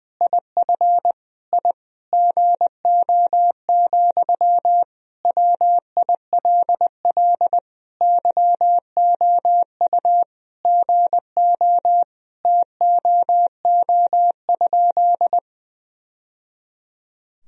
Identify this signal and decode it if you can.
Morse code, heard as a keyed tone, about 20 wpm, 700 Hz. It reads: IF I GO, WILL YOU GO TOO?
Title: Sentences from Top 1000 Words - Review Part 2 - 20wpm